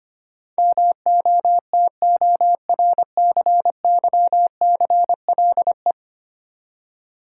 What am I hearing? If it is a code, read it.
Morse code, heard as a keyed tone, about 25 wpm, 700 Hz. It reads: MOTORCYCLE
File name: 10 Letter Words - First 3 Letters Given - 25wpm